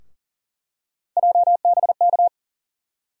Morse code, heard as a keyed tone, about 40 wpm, 700 Hz. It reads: JBK
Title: USA Call Sign Postfixes - 3 Letters 40wpm